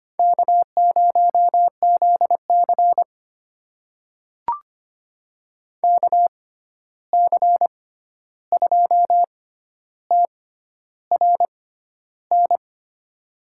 Morse code, heard as a keyed tone, about 25 wpm, spaced as for 11 wpm, 700 Hz. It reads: K0ZC KC2TRN
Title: USA Call Signs - 6x Character Spacing - 25wpm